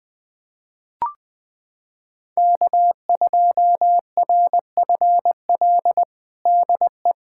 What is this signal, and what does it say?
Morse code, heard as a keyed tone, about 20 wpm, 700 Hz. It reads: K2RFL DE
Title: USA Callsign DE Callsign - 5 Characters - 20wpm